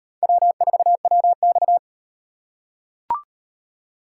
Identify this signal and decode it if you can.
Morse code, heard as a keyed tone, about 34 wpm, 700 Hz. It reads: W4WX